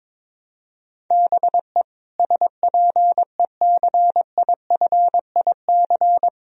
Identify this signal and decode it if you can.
Morse code, heard as a keyed tone, about 22 wpm, 700 Hz. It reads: BE SPECIFIC
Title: Sentences from Top 1000 Words - Review Part 3 - 22wpm